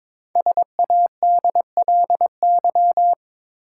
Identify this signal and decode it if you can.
Morse code, heard as a keyed tone, about 23 wpm, 690 Hz. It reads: SADLY
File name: 5 Letter Words - First 3 Letters Given - 22wpm